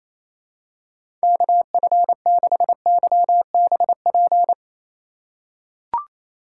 Morse code, heard as a keyed tone, about 28 wpm, 700 Hz. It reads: KF6YBP
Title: USA Call Signs - 6 Characters - 6x Character Spacing - 28wpm